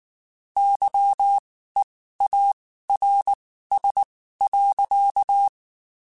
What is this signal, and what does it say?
Morse code, heard as a keyed tone, about 19 wpm, spaced as for 14 wpm, 800 Hz. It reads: YEARS.
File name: (spaced) / QSOs Spaced-mp3s / WA7QOX 2 @18 sp